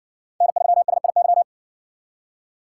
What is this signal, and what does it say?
Morse code, heard as a keyed tone, about 57 wpm, 700 Hz. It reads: N3SIY